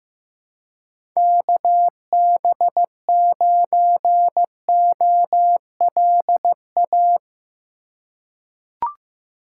Morse code, heard as a keyed tone, about 15 wpm, 700 Hz. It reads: KB9OLA
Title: USA Call Signs - 3x Character Spacing - 15wpm